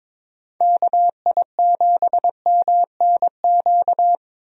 Morse code, heard as a keyed tone, about 22 wpm, 700 Hz. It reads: KI7MNQ
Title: USA Call Signs - 6 Characters - Repeated 2x - 22wpm